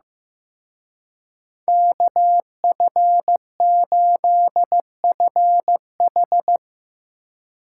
Morse code, heard as a keyed tone, about 17 wpm, 700 Hz. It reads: KF8FH
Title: Speed Racing - USA 2x2 Call Signs 15wpm